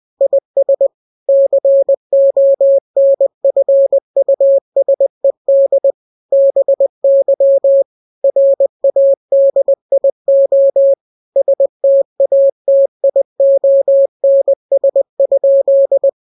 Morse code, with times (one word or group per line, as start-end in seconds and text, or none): 0.2-0.9 IS
1.3-5.9 CONFUSED
6.3-7.8 BY
8.2-10.9 RADIO
11.4-16.1 STATIONS?